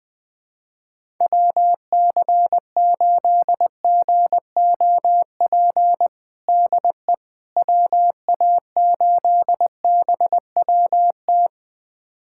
Morse code, heard as a keyed tone, about 20 wpm, 700 Hz. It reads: WC8GOP DE WA8BWT
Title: USA Callsign DE Callsign 20wpm